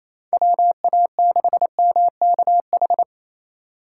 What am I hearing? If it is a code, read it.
Morse code, heard as a keyed tone, about 28 wpm, 700 Hz. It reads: WA6MKH
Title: USA Call Signs - 6 Characters - 4x Character Spacing - 28wpm